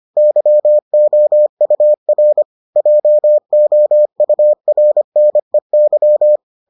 Morse code, heard as a keyed tone, about 25 wpm, 600 Hz. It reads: YOUR JOURNEY